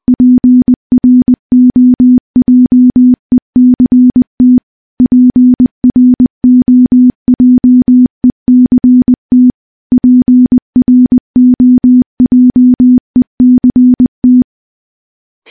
Morse code, heard as a keyed tone, about 20 wpm, 250 Hz. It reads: PROJECT PROJECT PROJECT
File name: morse-training-20WPM